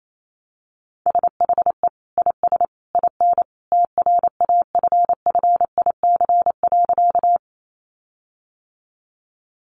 Morse code, heard as a keyed tone, about 28 wpm, 700 Hz. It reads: SHE IS IN TRAFFIC.
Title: Sentences from Top 1300 Words 28wpm